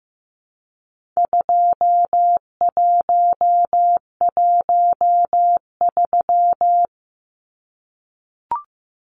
Morse code, heard as a keyed tone, about 15 wpm, 700 Hz. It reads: 2113
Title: Numbers - 4-Digit 15wpm